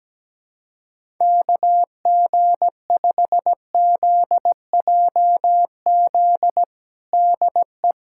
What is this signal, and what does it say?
Morse code, heard as a keyed tone, about 17 wpm, 700 Hz. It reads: KG5ZJZ DE
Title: USA Callsign DE Callsign - 6 Characters - 17wpm